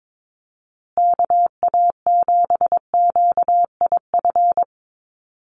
Morse code, heard as a keyed tone, about 22 wpm, 700 Hz. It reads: KA7QIF
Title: USA Call Signs Repeated 2x 22wpm